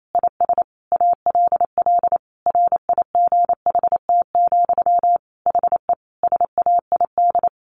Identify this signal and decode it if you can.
Morse code, heard as a keyed tone, about 28 wpm, 700 Hz. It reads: IS ALL RIGHT, HE SAID
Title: Sentences from Top 1100 Words 28wpm